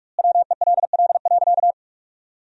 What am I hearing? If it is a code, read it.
Morse code, heard as a keyed tone, about 41 wpm, 700 Hz. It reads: WELL.
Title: 2x Word-Spacing - Sentences from Top 500 Words - 45wpm